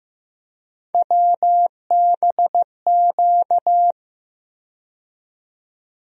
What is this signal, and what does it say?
Morse code, heard as a keyed tone, about 15 wpm, 700 Hz. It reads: WBQ
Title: USA Call Sign Postfixes - 3 Letters 15wpm